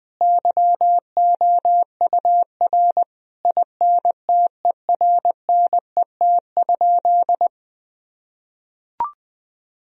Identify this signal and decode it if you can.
Morse code, heard as a keyed tone, about 21 wpm, 700 Hz. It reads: YOUR INTERNET?
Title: Sentences from Top 1000 Words - Review Part 4 - 20wpm